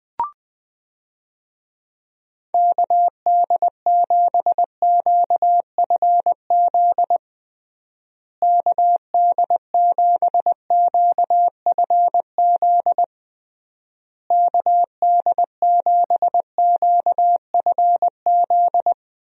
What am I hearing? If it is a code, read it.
Morse code, heard as a keyed tone, about 20 wpm, 700 Hz. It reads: KD7QFZ KD7QFZ KD7QFZ
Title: USA Call Signs - 6 Characters - Repeated 3x - 20wpm